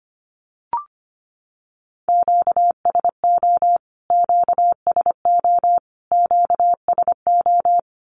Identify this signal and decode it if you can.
Morse code, heard as a keyed tone, about 25 wpm, 700 Hz. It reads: QSO QSO QSO
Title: Q-Codes Repeated 3x 25wpm